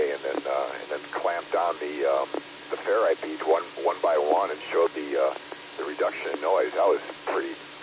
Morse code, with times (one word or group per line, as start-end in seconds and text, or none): none